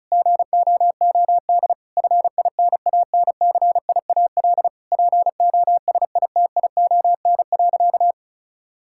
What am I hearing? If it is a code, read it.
Morse code, heard as a keyed tone, about 34 wpm, 700 Hz. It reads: GOOD FINANCIAL POSITION.
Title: Sentences from Top 1000 Words - Review Part 5 - 35wpm